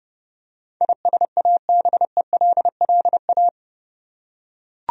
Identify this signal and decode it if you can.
Morse code, heard as a keyed tone, about 30 wpm, 700 Hz. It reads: ISABELLA
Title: USA Common Names - 30wpm